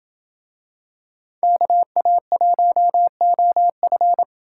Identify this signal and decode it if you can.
Morse code, heard as a keyed tone, about 27 wpm, 700 Hz. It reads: KA1OF